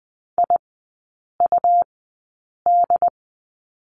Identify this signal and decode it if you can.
Morse code, heard as a keyed tone, about 20 wpm, spaced as for 8 wpm, 700 Hz. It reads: IUD
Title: 114 Lesson 19e - Letter w - Three Characters 20wpm